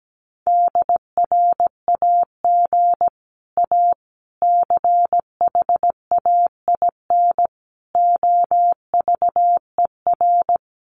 Morse code, heard as a keyed tone, about 17 wpm, 700 Hz. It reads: DRAG A CHAIN OVER
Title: Sentences from Top 1500 Words 17wpm